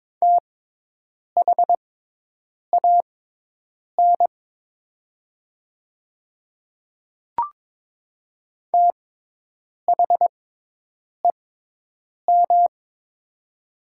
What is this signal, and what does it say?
Morse code, heard as a keyed tone, about 22 wpm, spaced as for 8 wpm, 700 Hz. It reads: THAN THEM
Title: 6x Character Spacing - Top 200 Words - 22wpm